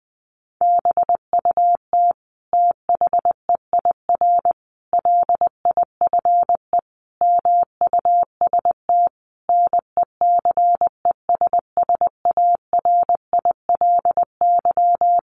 Morse code, heard as a keyed tone, about 20 wpm, 700 Hz. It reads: BUT THEIR LIFE MUST NECESSARILY
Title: Sentences from Top 1400 Words 20wpm